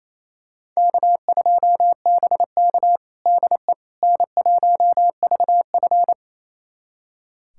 Morse code, heard as a keyed tone, about 28 wpm, 700 Hz. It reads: K2BK DE N1VF
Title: USA Callsign DE Callsign - 4 Characters - 28wpm